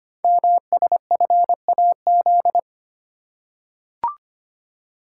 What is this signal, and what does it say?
Morse code, heard as a keyed tone, about 25 wpm, 700 Hz. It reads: MSFAZ
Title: Code Groups - 5 Letters - 25wpm